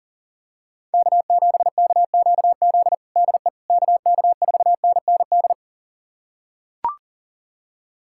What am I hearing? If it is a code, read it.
Morse code, heard as a keyed tone, about 40 wpm, 700 Hz. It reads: K7KQZ DE KK4NND
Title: USA Callsign DE Callsign 40wpm